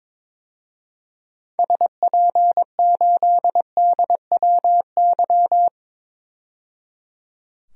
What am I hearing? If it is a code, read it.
Morse code, heard as a keyed tone, about 22 wpm, 700 Hz. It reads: SP8DWY
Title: DXCC-100 Call Signs - 6 Characters - 22wpm